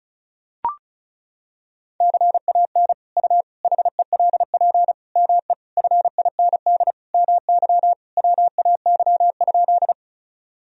Morse code, heard as a keyed tone, about 36 wpm, 700 Hz. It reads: CAN U HELP ME FIND MY WAY?
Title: Sentences from Top 200 Words and Abbreviations 35wpm